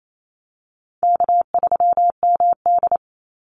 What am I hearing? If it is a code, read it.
Morse code, heard as a keyed tone, about 28 wpm, 700 Hz. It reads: K3MD
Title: Easy Call Signs - 28wpm